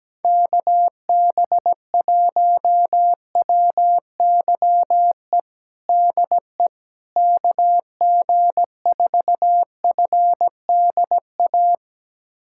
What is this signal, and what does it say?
Morse code, heard as a keyed tone, about 17 wpm, 700 Hz. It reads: KB1WYE DE KG4FDA
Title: USA Callsign DE Callsign - 6 Characters - 17wpm